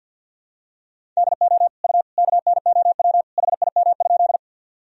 Morse code, heard as a keyed tone, about 52 wpm, 700 Hz. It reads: DO U KNOW HIM?
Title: Sentences from Top 100 Words and Abbreviations 50wpm